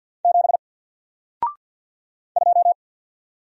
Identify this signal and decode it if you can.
Morse code, heard as a keyed tone, about 50 wpm, 700 Hz. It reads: 7 2
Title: Single Character (Letter:Number:Punctuation:Prosign) Alt 50wpm